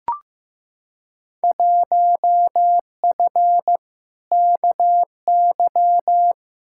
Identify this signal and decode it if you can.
Morse code, heard as a keyed tone, about 15 wpm, 700 Hz. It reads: 1F KY
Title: Field Day Exchanges Encore - 15wpm